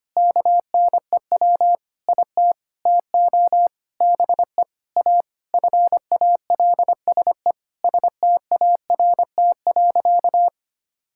Morse code, heard as a keyed tone, about 25 wpm, 700 Hz. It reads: KNEW IT TO BE A FALSE START.
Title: Sentences from Top 1600 Words 25wpm